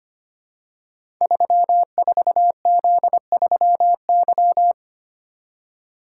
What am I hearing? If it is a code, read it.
Morse code, heard as a keyed tone, about 25 wpm, 700 Hz. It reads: 34Z3Y
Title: Code Groups - 5 Letter:Numbers - 25wpm